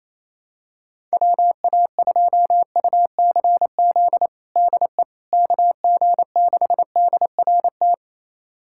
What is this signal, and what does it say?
Morse code, heard as a keyed tone, about 28 wpm, 700 Hz. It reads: WA2UCZ DE KG6DRT
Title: USA Callsign DE Callsign - 6 Characters - 28wpm